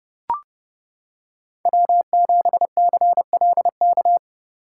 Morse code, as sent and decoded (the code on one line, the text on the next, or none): .-- --... -.-. .-.. -.-
W7CLK